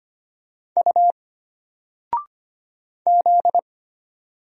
Morse code, heard as a keyed tone, about 25 wpm, 700 Hz. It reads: U Z
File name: Single-Letter-Number-25wpm